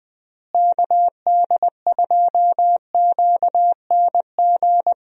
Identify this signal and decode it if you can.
Morse code, heard as a keyed tone, about 21 wpm, 700 Hz. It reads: KD2QNG
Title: USA Call Signs - 6 Characters - Repeated 3x - 20wpm